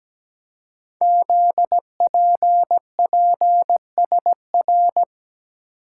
Morse code, heard as a keyed tone, about 17 wpm, 700 Hz. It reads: ZPPSR
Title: Code Groups - 5 Letters - 17wpm